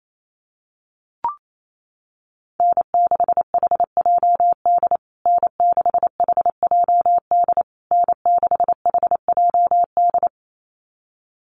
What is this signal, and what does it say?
Morse code, heard as a keyed tone, about 28 wpm, 700 Hz. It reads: N6HJD N6HJD N6HJD
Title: USA Call Signs Repeated 3x 28wpm